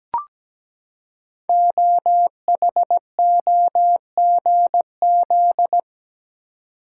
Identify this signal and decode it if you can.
Morse code, heard as a keyed tone, about 17 wpm, 700 Hz. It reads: OHOGZ
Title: Code Groups - 5 Letters - 17wpm